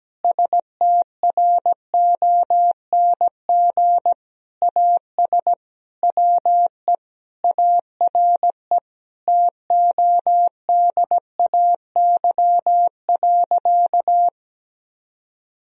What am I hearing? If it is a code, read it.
Morse code, heard as a keyed tone, about 17 wpm, 700 Hz. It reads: STRONG AS WE ARE TODAY.
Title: Sentences from Top 1000 Words - Review Part 4 - 17wpm